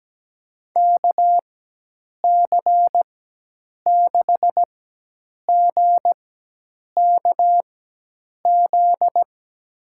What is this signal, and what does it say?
Morse code, heard as a keyed tone, about 17 wpm, spaced as for 8 wpm, 700 Hz. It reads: KC6GKZ
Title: USA Call Signs - 6 Characters - 4x Character Spacing - 17wpm